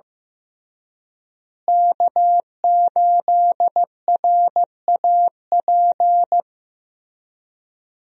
Morse code, heard as a keyed tone, about 16 wpm, 700 Hz. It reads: K8RAP
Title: USA Call Signs - 5 Characters - Repeated 3x - 15wpm